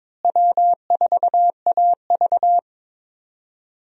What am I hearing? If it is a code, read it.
Morse code, heard as a keyed tone, about 22 wpm, 700 Hz. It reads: W4AV
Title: USA Call Signs - 4 Characters - Repeated 3x - 22wpm